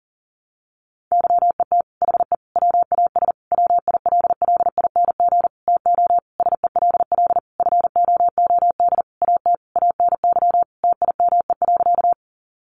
Morse code, heard as a keyed tone, about 40 wpm, 700 Hz. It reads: YET HE WAS WILLING TO SELL FOOD AT ANY TIME.
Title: Sentences from Top 1100 Words 40wpm